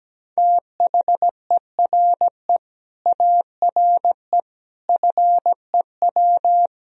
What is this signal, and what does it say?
Morse code, heard as a keyed tone, about 17 wpm, 700 Hz. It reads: THERE ARE FEW
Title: Sentences from Top 1000 Words - Review Part 1 - 17wpm